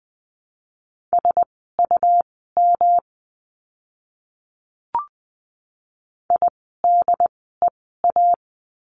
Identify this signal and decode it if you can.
Morse code, heard as a keyed tone, about 20 wpm, spaced as for 14 wpm, 700 Hz. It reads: SUM IDEA
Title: 108 Lesson 18f - Letter m - Words 20wpm